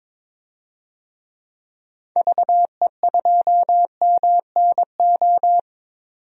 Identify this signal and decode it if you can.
Morse code, heard as a keyed tone, about 22 wpm, 700 Hz. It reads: VE2MNO
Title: DXCC-100 Call Signs - 6 Characters - 22wpm